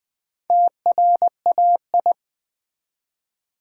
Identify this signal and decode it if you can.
Morse code, heard as a keyed tone, about 20 wpm, 700 Hz. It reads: TRAI
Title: Build A Word - 1 Letter At A Time - 5-Letter Words - 20wpm